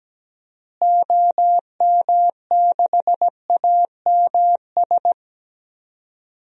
Morse code, heard as a keyed tone, about 17 wpm, 700 Hz. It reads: OM6AMS
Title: DXCC-100 Call Signs - 6 Characters - Repeated 2x - 17wpm